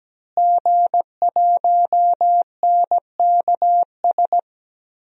Morse code, heard as a keyed tone, about 17 wpm, 700 Hz. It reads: G1NKS
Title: DXCC-100 Call Signs - 5 Characters - Repeated 3x - 17wpm